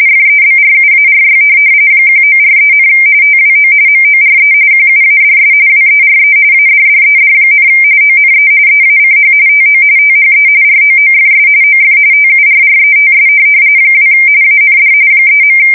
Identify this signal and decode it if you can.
RTTY, about 45.45 baud. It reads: FEBRUARY 1943, TWO MORE QUADS WERE ADDED, ONE EACH TO THE ROOFS OF TURRETS 2 AND 3.
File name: fsk170 / USS-Massachusetts